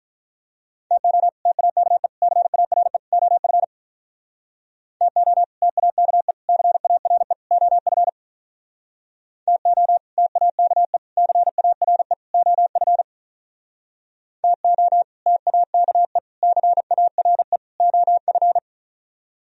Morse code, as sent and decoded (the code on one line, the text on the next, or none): - --- / - .- -.- . / -.-. .- .-. / --- ..-. / - --- / - .- -.- . / -.-. .- .-. / --- ..-. / - --- / - .- -.- . / -.-. .- .-. / --- ..-. / - --- / - .- -.- . / -.-. .- .-. / --- ..-.
TO TAKE CARE OF TO TAKE CARE OF TO TAKE CARE OF TO TAKE CARE OF